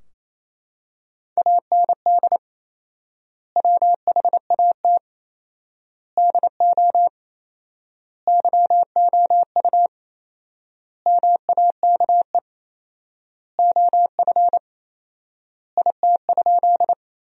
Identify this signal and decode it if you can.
Morse code, heard as a keyed tone, about 28 wpm, 700 Hz. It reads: AND WHAT DO YOU MAKE OF IT?
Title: 4x Word-Spacing - Sentences from Top 100 Words - 28wpm